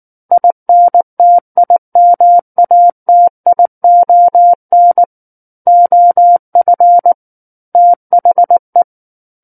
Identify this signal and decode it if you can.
Morse code, heard as a keyed tone, about 19 wpm, 700 Hz. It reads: INTIMATION OF THE
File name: Book One / 10-Chapter 10 - 19 WPM